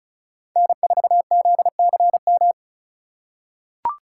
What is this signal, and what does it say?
Morse code, heard as a keyed tone, about 36 wpm, 700 Hz. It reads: N4ZCM
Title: USA Call Signs - 5 Characters - 4x Character Spacing - 35wpm